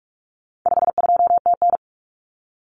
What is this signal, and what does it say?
Morse code, heard as a keyed tone, about 44 wpm, 700 Hz. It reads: 52TN
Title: Common Cut Numbers - 4-Digit 45wpm